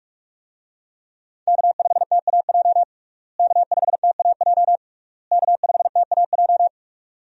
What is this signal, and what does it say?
Morse code, heard as a keyed tone, about 47 wpm, 700 Hz. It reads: K5TAJ K5TAJ K5TAJ
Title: USA Call Signs - 5 Characters - Repeated 3x - 45wpm